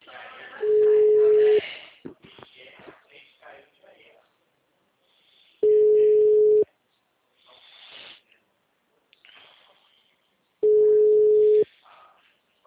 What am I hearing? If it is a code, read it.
Ringback tone (Europe, 425 Hz), with speech.